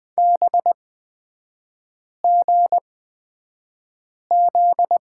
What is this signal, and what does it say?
Morse code, heard as a keyed tone, about 20 wpm, spaced as for 5 wpm, 700 Hz. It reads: BGZ
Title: 243 Lesson 39e - Letter z - Three Characters 20wpm